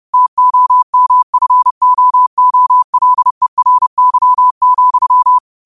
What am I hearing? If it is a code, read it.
Morse code, heard as a keyed tone, about 30 wpm, 1000 Hz. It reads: TOMFOOLERY,
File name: qotdfiles / 30wpm